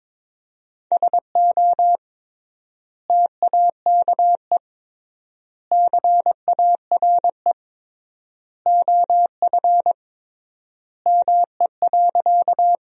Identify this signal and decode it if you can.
Morse code, heard as a keyed tone, about 21 wpm, 700 Hz. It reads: SO TAKE CARE OF ME.
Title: 3x Word-Spacing - Sentences from Top 300 Words - 22wpm